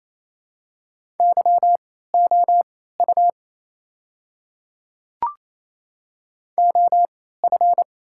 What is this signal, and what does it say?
Morse code, heard as a keyed tone, about 27 wpm, spaced as for 16 wpm, 690 Hz. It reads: YOU OF
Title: 3x Character Spacing - Top 100 Words - 28wpm